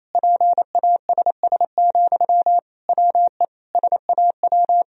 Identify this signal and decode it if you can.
Morse code, heard as a keyed tone, about 28 wpm, 700 Hz. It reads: PASS, WE SAW
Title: Sentences from Top 1000 Words - Review Part 2 - 28wpm